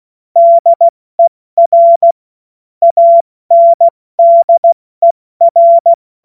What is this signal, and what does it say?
Morse code, heard as a keyed tone, about 16 wpm, 680 Hz. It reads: DER ANDER
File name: Morsekurs / Witze / Witz 31.Oktober 2023 70 F80